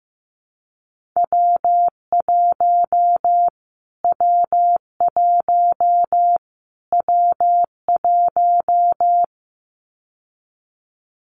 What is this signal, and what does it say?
Morse code, heard as a keyed tone, about 15 wpm, 700 Hz. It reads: W1 W1 W1
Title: Callsign Prefixes for Top 25 DXCC - Repeated 3x - 15wpm